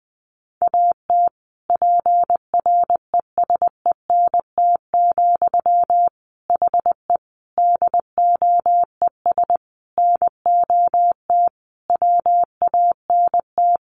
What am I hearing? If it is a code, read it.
Morse code, heard as a keyed tone, about 20 wpm, 700 Hz. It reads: AT PRESENT, HE DOES NOT WANT
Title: Sentences from Top 800 Words 20wpm